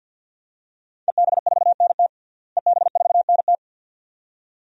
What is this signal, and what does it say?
Morse code, heard as a keyed tone, about 47 wpm, 700 Hz. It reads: EB4NT EB4NT
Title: DXCC-100 Call Signs - 5 Characters - Repeated 2x - 50wpm